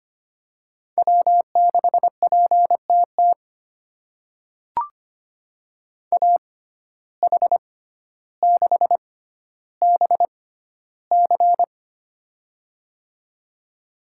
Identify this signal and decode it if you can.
Morse code, heard as a keyed tone, about 25 wpm, spaced as for 13 wpm, 700 Hz. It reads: W6PTT AH6BC
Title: USA Call Signs - 5 Characters - 6x Character Spacing - 25wpm